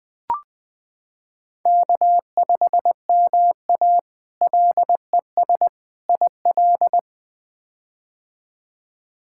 Morse code, heard as a keyed tone, about 20 wpm, 700 Hz. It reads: K5MA LES IL
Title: K1USN SST QSO Practice - 20wpm